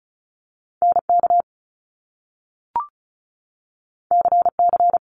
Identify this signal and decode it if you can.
Morse code, heard as a keyed tone, about 36 wpm, 700 Hz. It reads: N K C C
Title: Top Two Letter Combinations in Words 35wpm